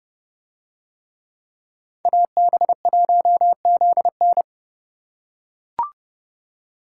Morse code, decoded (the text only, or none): AB1ZN